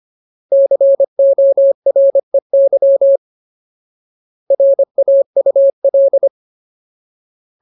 Morse code, heard as a keyed tone, about 25 wpm, 550 Hz. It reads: COREY RAUL
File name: Ultimate Word List / 25WPM Ultimate